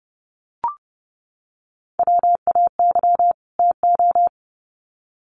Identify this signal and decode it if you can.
Morse code, heard as a keyed tone, about 30 wpm, 700 Hz. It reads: WAY TO
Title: Sets of 2 Words 30wpm